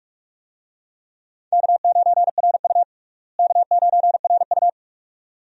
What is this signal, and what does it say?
Morse code, heard as a keyed tone, about 46 wpm, 700 Hz. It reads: K9RU K9RU
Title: USA Call Signs - 4 Characters - Repeated 2x - 45wpm